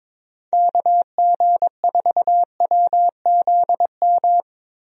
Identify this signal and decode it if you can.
Morse code, heard as a keyed tone, about 22 wpm, 700 Hz. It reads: KG4WZM
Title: USA Call Signs - 6 Characters - 22wpm